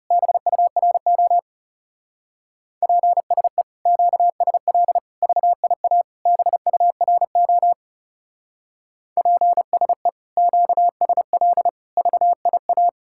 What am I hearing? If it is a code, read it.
Morse code, heard as a keyed tone, about 36 wpm, 700 Hz. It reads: BURO PSE QSL VIA BURO PSE QSL VIA
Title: Speed-Racing Common QSO Phrases Encore - 30wpm